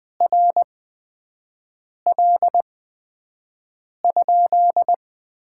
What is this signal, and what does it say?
Morse code, heard as a keyed tone, about 20 wpm, spaced as for 5 wpm, 700 Hz. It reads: RL?
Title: Top 90 QSO Elements - 8x Character Spacing - 20wpm